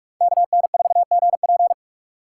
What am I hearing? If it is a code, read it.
Morse code, heard as a keyed tone, about 47 wpm, 700 Hz. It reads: KN4GP
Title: USA Call Signs - 4x Character Spacing - 45wpm